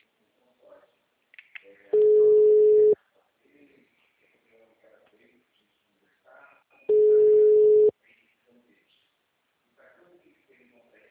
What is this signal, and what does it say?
Ringback tone (Europe, 425 Hz), with speech.